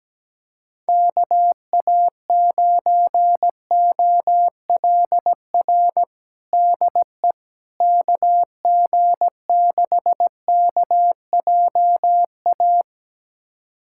Morse code, heard as a keyed tone, about 17 wpm, 700 Hz. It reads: KA9OLR DE KG6KJA
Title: USA Callsign DE Callsign - 6 Characters - 17wpm